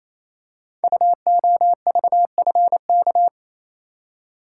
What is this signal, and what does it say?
Morse code, heard as a keyed tone, about 27 wpm, 700 Hz. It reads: UOVFK
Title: Code Groups - 5 Letters - 28wpm